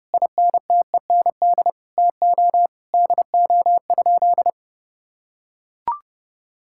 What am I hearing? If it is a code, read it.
Morse code, heard as a keyed tone, about 30 wpm, 700 Hz. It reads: INTEND TO DO?
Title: Sentences from Top 1600 Words 30wpm